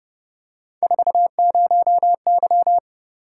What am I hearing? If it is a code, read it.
Morse code, heard as a keyed tone, about 31 wpm, 700 Hz. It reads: 40Y